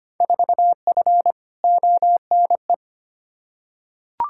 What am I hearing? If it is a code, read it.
Morse code, heard as a keyed tone, about 25 wpm, 700 Hz. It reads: 4F ONE
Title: Field Day Exchanges Encore - 25wpm